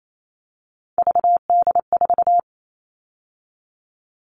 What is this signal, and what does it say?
Morse code, heard as a keyed tone, about 28 wpm, 700 Hz. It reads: VD4
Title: All Cut Numbers - 3-Digit 28wpm